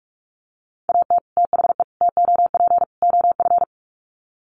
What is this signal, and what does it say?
Morse code, heard as a keyed tone, about 44 wpm, 700 Hz. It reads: AT THE TOP OF
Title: Sets of 4 Words 45wpm